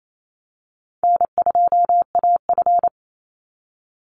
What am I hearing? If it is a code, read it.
Morse code, heard as a keyed tone, about 28 wpm, 700 Hz. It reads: N2AF
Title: NA SOTA Chaser Callsigns 28wpm